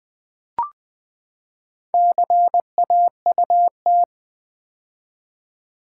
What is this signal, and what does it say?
Morse code, heard as a keyed tone, about 20 wpm, 700 Hz. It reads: CAUT
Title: Build A Word - 4 Letters At A Time - 8-Letter Words - 20wpm